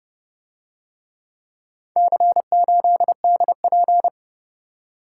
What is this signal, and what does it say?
Morse code, heard as a keyed tone, about 28 wpm, 700 Hz. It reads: C8DP